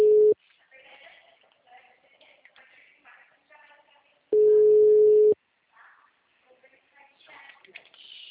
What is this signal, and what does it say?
Ringback tone (Europe, 425 Hz).